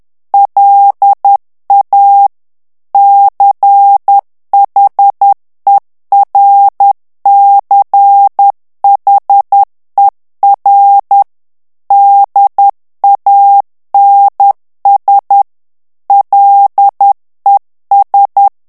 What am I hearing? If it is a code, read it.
Morse code, heard as a keyed tone, about 10 wpm, 800 Hz. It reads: LA CHERCHER DANS LES
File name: morse mp3 / FRANCAIS TEXTE CLAIR 2 VITESSE 10 MOTS MINUTES